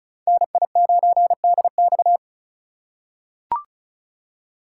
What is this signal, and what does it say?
Morse code, heard as a keyed tone, about 36 wpm, 700 Hz. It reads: NI9DX
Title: USA Call Signs - 3x Character Spacing - 35wpm